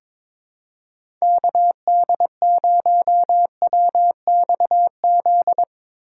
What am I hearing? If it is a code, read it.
Morse code, heard as a keyed tone, about 22 wpm, 700 Hz. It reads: KD0WXZ
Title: USA Call Signs - 6 Characters - 3x Character Spacing - 22wpm